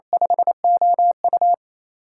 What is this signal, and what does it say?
Morse code, heard as a keyed tone, about 28 wpm, 700 Hz. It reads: 5OU
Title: USA Call Signs - 6x Character Spacing - 28wpm